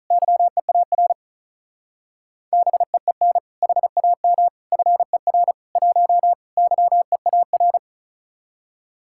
Morse code, heard as a keyed tone, about 42 wpm, 700 Hz. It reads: YEAR BEEN HAM FER 1 YEAR
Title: Speed-Racing Common QSO Phrases - 35wpm